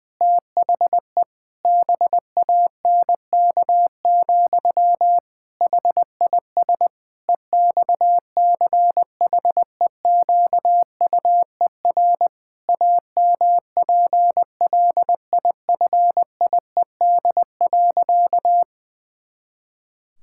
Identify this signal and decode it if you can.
Morse code, heard as a keyed tone, about 20 wpm, 700 Hz. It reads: THE BANK, HIS EXCHEQUER AMPLIFIED.
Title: Pangram Challenge v2 - 20wpm